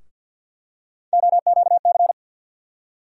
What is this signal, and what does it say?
Morse code, heard as a keyed tone, about 52 wpm, 700 Hz. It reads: OQC
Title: USA Call Sign Postfixes - 3 Letters 50wpm